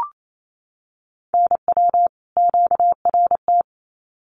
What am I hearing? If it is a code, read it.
Morse code, heard as a keyed tone, about 27 wpm, 700 Hz. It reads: NW QRT
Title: Common QSO Phrases 28wpm